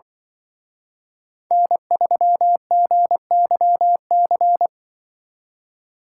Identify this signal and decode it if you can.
Morse code, heard as a keyed tone, about 24 wpm, 700 Hz. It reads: N3GYC
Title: Speed Racing - USA 1x3 Call Signs 20wpm